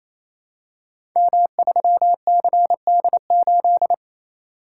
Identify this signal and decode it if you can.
Morse code, heard as a keyed tone, about 28 wpm, 700 Hz. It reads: M3CD8
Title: Code Groups - 5 Letter:Numbers - 28wpm